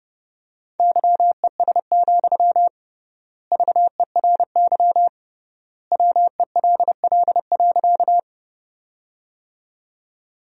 Morse code, heard as a keyed tone, about 30 wpm, 700 Hz. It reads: YES, VERY WELL.
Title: 3x Word-Spacing - Sentences from Top 500 Words - 30wpm